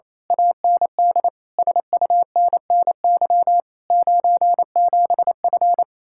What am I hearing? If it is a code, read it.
Morse code, heard as a keyed tone, about 26 wpm, 700 Hz. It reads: AND SUNNY 97F
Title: Speed-Racing Common QSO Phrases - 28wpm